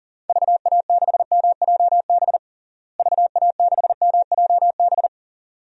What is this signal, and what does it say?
Morse code, heard as a keyed tone, about 40 wpm, 700 Hz. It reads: VA6MJB VA6MJB
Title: DXCC-100 Call Signs - 6 Characters - Repeated 2x - 40wpm